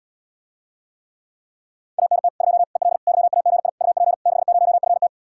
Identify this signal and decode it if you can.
Morse code, heard as a keyed tone, about 80 wpm, 700 Hz. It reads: ANT 2 EL YAGI UP 60FT